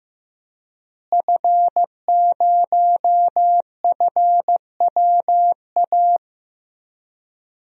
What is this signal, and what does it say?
Morse code, heard as a keyed tone, about 15 wpm, 700 Hz. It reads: F0FWA
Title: DXCC-100 Call Signs - 5 Characters - Repeated 2x - 15wpm